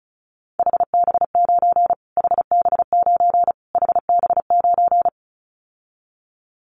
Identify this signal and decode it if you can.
Morse code, heard as a keyed tone, about 34 wpm, 700 Hz. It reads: HB9 HB9 HB9
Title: Callsign Prefixes for Top 25 DXCC - Repeated 3x - 35wpm